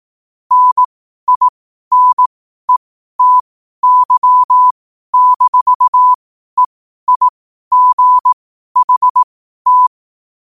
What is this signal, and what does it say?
Morse code, heard as a keyed tone, about 18 wpm, spaced as for 12 wpm, 1000 Hz. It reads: NINETY-EIGHT
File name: qotdfiles / 12wpm